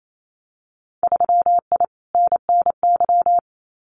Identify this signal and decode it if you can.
Morse code, heard as a keyed tone, about 28 wpm, 700 Hz. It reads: 3I NNY
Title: Winter Field Day Exchanges 28wpm